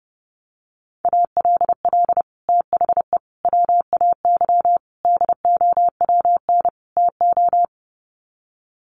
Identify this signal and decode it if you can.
Morse code, heard as a keyed tone, about 30 wpm, 700 Hz. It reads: ALL THE WAY DOWN TO
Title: Sets of 5 Words Encore2 30wpm